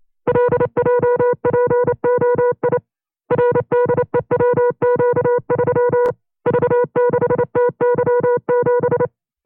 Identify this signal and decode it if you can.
Morse code, heard as a keyed tone, about 28 wpm, 490 Hz. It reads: LJPOI RDEWQ3 V6TY7
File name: FreeFiles / CW-practice / 25WPM-side1